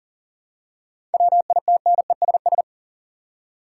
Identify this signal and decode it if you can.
Morse code, heard as a keyed tone, about 40 wpm, 700 Hz. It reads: WITNESS
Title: Seven Letter Words - 40wpm